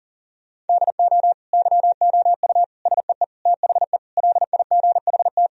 Morse code, heard as a keyed tone, about 37 wpm, 700 Hz. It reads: DO YOU SEE THE LIGHT
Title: Sentences from Top 1000 Words - Review Part 1 - 40wpm